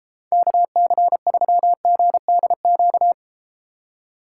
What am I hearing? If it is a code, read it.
Morse code, heard as a keyed tone, about 32 wpm, 700 Hz. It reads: KC3GDQ